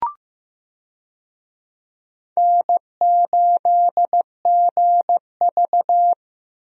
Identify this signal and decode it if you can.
Morse code, heard as a keyed tone, about 15 wpm, 700 Hz. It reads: N8GV
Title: USA Call Signs - 4 Characters - Repeated 2x - 15wpm